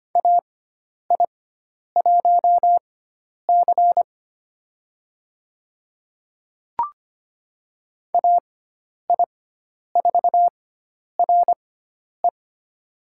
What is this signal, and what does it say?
Morse code, heard as a keyed tone, about 25 wpm, spaced as for 10 wpm, 700 Hz. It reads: AI1C AI4RE
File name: Lesson 16g - Letter c - Call Signs 25wpm